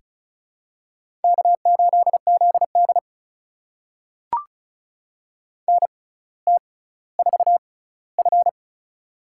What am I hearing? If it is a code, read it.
Morse code, heard as a keyed tone, about 36 wpm, spaced as for 18 wpm, 700 Hz. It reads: K8ZD NT4F
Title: USA Call Signs - 4 Characters - 6x Character Spacing - 35wpm